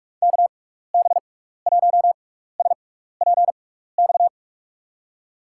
Morse code, heard as a keyed tone, about 46 wpm, spaced as for 16 wpm, 700 Hz. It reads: KB1SPX
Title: USA Call Signs - 6 Characters - 6x Character Spacing - 45wpm